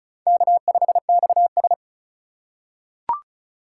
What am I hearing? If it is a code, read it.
Morse code, heard as a keyed tone, about 34 wpm, 700 Hz. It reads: K5XS
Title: USA Call Signs - 4 Characters - 3x Character Spacing - 35wpm